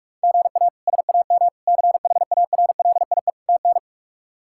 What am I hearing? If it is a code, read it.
Morse code, heard as a keyed tone, about 46 wpm, 700 Hz. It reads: GA SAM CHARLIE TN
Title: K1USN SST QSO Practice - 45wpm